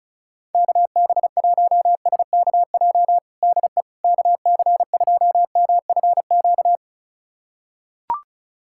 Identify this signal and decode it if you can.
Morse code, heard as a keyed tone, about 34 wpm, 700 Hz. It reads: KB1SKJ DE KC2MFQ
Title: USA Callsign DE Callsign 35wpm